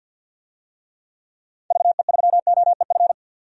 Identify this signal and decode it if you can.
Morse code, heard as a keyed tone, about 52 wpm, 700 Hz. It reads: VE3OEF
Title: DXCC-100 Call Signs - 6 Characters - 50wpm